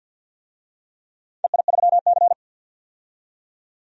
Morse code, heard as a keyed tone, about 46 wpm, 700 Hz. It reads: EI3C
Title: DXCC-100 Call Signs - 4 Characters - 50wpm